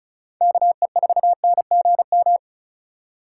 Morse code, heard as a keyed tone, about 36 wpm, 700 Hz. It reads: KE4NGM
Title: USA Call Signs - 6 Characters - 4x Character Spacing - 35wpm